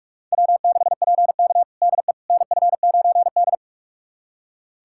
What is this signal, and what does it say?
Morse code, heard as a keyed tone, about 42 wpm, 700 Hz. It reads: W6PK DE NF9D
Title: USA Callsign DE Callsign - 4 Characters - 45wpm